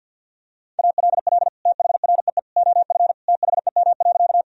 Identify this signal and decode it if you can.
Morse code, heard as a keyed tone, about 52 wpm, 700 Hz. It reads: ALL THREE OF THEM.
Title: Sentences from Top 1000 Words - Review Part 1 - 50wpm